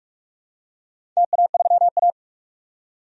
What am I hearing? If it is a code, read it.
Morse code, heard as a keyed tone, about 47 wpm, 700 Hz. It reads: TA3A
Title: DXCC-100 Call Signs - 4 Characters - 45wpm